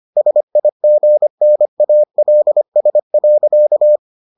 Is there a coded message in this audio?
Yes: Morse code from 0.2 to 4.0 s, heard as a keyed tone, about 25 wpm, 600 Hz. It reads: SIGNALS.